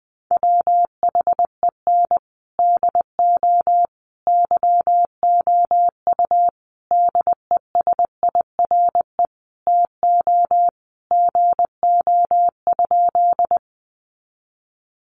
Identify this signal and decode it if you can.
Morse code, heard as a keyed tone, about 20 wpm, 700 Hz. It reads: WHEN DO YOU DESIRE TO GO?
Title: Sentences from Top 1100 Words 20wpm